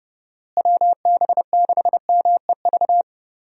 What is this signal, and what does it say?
Morse code, heard as a keyed tone, about 30 wpm, 700 Hz. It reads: WB6MEV
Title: USA Call Signs - 6 Characters - Repeated 3x - 30wpm